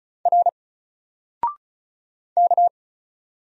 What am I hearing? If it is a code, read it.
Morse code, heard as a keyed tone, about 35 wpm, 700 Hz. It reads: R K
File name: Single-Letter-Number-35wpm